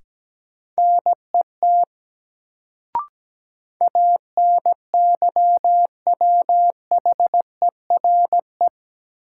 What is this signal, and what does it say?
Morse code, heard as a keyed tone, about 17 wpm, 700 Hz. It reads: NET ANYWHERE
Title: Sentences from Top 1400 Words - New Word Drill - 17wpm